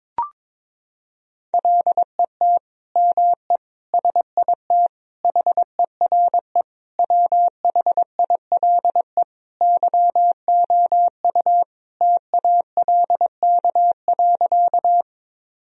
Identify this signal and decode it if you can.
Morse code, heard as a keyed tone, about 22 wpm, 700 Hz. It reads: LET ME SIT HERE WHILE YOU TALK.
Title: Sentences from Top 1000 Words - Review Part 1 - 22wpm